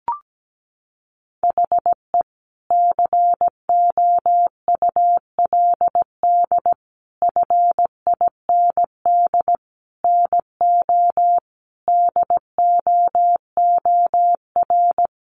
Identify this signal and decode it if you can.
Morse code, heard as a keyed tone, about 17 wpm, 700 Hz. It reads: HE COULD FIND NO DOOR
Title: Sentences from Top 1100 Words 17wpm